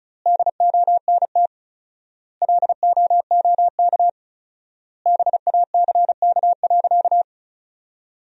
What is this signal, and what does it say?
Morse code, heard as a keyed tone, about 36 wpm, 700 Hz. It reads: DONT LOOK BACK.
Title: 4x Word-Spacing - Sentences from Top 300 Words - 35wpm